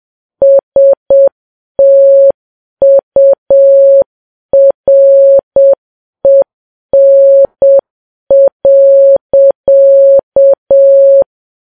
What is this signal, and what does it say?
Morse code, heard as a keyed tone, about 7 wpm, 550 Hz. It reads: STUREN.
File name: Files Web tls / Morse on9fv / morse7kps